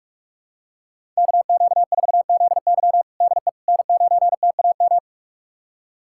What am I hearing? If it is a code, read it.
Morse code, heard as a keyed tone, about 46 wpm, 700 Hz. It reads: KQ4ZY DE N9TAM
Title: USA Callsign DE Callsign - 5 Characters - 45wpm